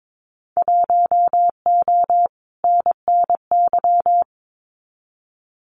Morse code, heard as a keyed tone, about 22 wpm, 700 Hz. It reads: 1O NNY
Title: Winter Field Day Exchanges 22wpm